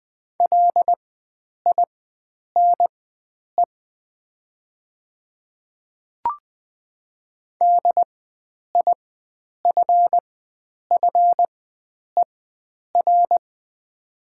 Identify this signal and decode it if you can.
Morse code, heard as a keyed tone, about 20 wpm, spaced as for 9 wpm, 700 Hz. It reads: LINE DIFFER
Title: 4x Character Spacing - Top 300 Words - 20wpm